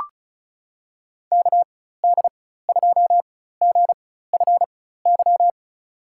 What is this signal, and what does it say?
Morse code, heard as a keyed tone, about 36 wpm, spaced as for 16 wpm, 700 Hz. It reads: KD2GFY
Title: USA Call Signs - 6 Characters - 4x Character Spacing - 35wpm